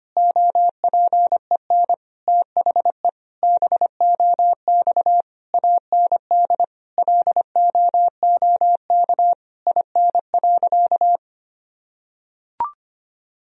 Morse code, heard as a keyed tone, about 25 wpm, 700 Hz. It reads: OPEN THE BOX AND LOOK IN.
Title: Sentences from Top 1000 Words - Review Part 2 - 25wpm